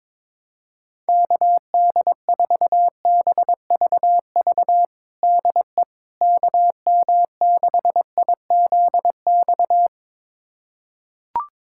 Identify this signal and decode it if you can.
Morse code, heard as a keyed tone, about 22 wpm, 700 Hz. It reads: KD4BVV DE KM6IZX
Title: USA Callsign DE Callsign - 6 Characters - 22wpm